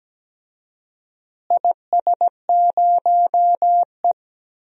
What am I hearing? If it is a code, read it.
Morse code, heard as a keyed tone, about 17 wpm, 700 Hz. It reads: IS0E
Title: DXCC-100 Call Signs - 4 Characters - Repeated 2x - 17wpm